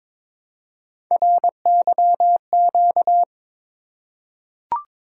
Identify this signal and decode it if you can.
Morse code, heard as a keyed tone, about 22 wpm, 700 Hz. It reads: RYQ
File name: practice / 22wpm / Three Letter-Number Combinations - 22wpm